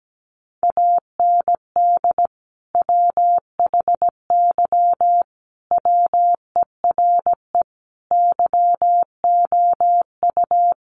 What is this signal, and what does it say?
Morse code, heard as a keyed tone, about 17 wpm, 700 Hz. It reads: AND WHY WERE YOU
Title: Sentences from Top 1000 Words 17wpm